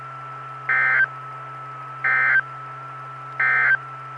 Emergency Alert System (SAME) tones, heard as harsh buzzing bursts. It is heard from 0.6 to 3.8 s.
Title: Saline, AK SKYWARN Repeater System - W5RHS